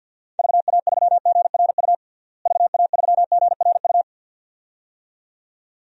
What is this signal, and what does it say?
Morse code, heard as a keyed tone, about 47 wpm, 700 Hz. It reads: VA3GRU VA3GRU
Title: DXCC-100 Call Signs - 6 Characters - Repeated 2x - 50wpm